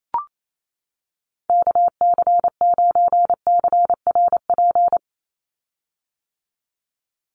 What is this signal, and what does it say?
Morse code, heard as a keyed tone, about 28 wpm, 700 Hz. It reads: KC9CRP
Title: USA Call Signs 28wpm